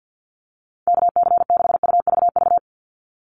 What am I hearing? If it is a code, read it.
Morse code, heard as a keyed tone, about 49 wpm, 700 Hz. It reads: KC6UVV
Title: USA Call Signs 50wpm